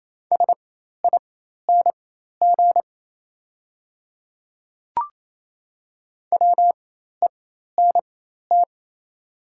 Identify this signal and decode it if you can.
Morse code, heard as a keyed tone, about 28 wpm, spaced as for 13 wpm, 700 Hz. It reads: SING WENT
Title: 4x Character Spacing - Top 500 Words - 28wpm